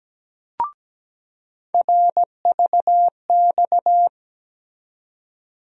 Morse code, heard as a keyed tone, about 17 wpm, 700 Hz. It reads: RVX
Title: USA Call Sign Postfixes - 3 Letters 17wpm